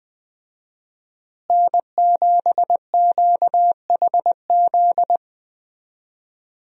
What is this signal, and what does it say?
Morse code, heard as a keyed tone, about 20 wpm, 700 Hz. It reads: N7QHZ
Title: USA Call Signs - 5 Characters - Repeated 2x - 20wpm